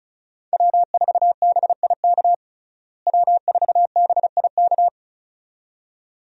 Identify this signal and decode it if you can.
Morse code, heard as a keyed tone, about 36 wpm, 700 Hz. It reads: W4BIK W4BIK
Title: USA Call Signs - 5 Characters - Repeated 2x - 35wpm